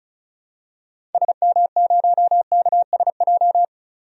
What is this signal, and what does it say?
Morse code, heard as a keyed tone, about 36 wpm, 700 Hz. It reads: SM0KSJ
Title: DXCC-100 Call Signs - 6 Characters - 35wpm